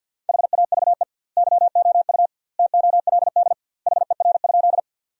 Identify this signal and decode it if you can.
Morse code, heard as a keyed tone, about 52 wpm, 700 Hz. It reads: HAVE YOU TOLD HER?
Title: 2x Word-Spacing - Sentences from Top 400 Words - 50wpm